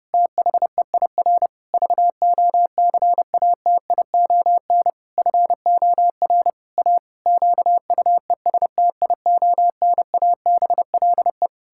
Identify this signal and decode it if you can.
Morse code, heard as a keyed tone, about 28 wpm, 700 Hz. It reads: THEIR VOCATION FOR A QUESTIONABLE
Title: Pangram Challenge v2 - 30wpm